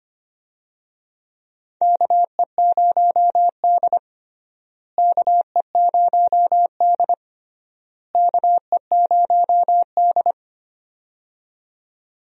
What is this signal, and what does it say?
Morse code, heard as a keyed tone, about 25 wpm, 700 Hz. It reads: KE0D KE0D KE0D
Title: USA Call Signs - 4 Characters - Repeated 3x - 25wpm